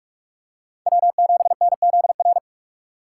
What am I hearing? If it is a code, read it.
Morse code, heard as a keyed tone, about 42 wpm, 700 Hz. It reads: W7NZR
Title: USA Call Signs - 5 Characters - 6x Character Spacing - 45wpm